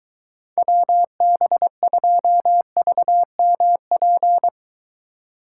Morse code, heard as a keyed tone, about 23 wpm, 700 Hz. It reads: WB2VMP